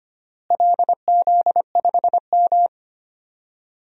Morse code, heard as a keyed tone, about 25 wpm, 700 Hz. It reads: LZ5M
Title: DXCC-100 Call Signs - 4 Characters - Repeated 3x - 25wpm